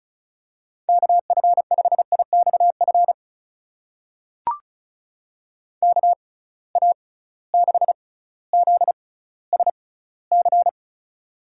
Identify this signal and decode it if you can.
Morse code, heard as a keyed tone, about 36 wpm, spaced as for 18 wpm, 700 Hz. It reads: KF5IXF KA6ZSC
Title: USA Call Signs - 6 Characters - 6x Character Spacing - 35wpm